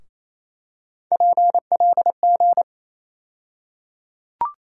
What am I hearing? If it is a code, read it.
Morse code, heard as a keyed tone, about 27 wpm, 700 Hz. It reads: PLG
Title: USA Call Sign Postfixes - 3 Letters 28wpm